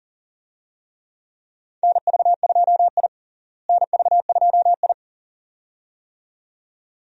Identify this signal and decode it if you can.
Morse code, heard as a keyed tone, about 40 wpm, 700 Hz. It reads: NV2I NV2I
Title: USA Call Signs - 4 Characters - Repeated 2x - 40wpm